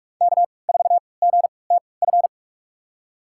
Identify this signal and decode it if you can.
Morse code, heard as a keyed tone, about 46 wpm, spaced as for 25 wpm, 700 Hz. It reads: K4GTF
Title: USA Call Signs - 3x Character Spacing - 45wpm